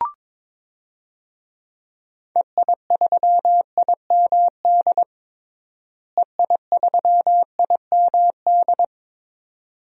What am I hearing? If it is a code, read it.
Morse code, heard as a keyed tone, about 22 wpm, 700 Hz. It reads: EI3IMD EI3IMD
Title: DXCC-100 Call Signs - 6 Characters - Repeated 2x - 22wpm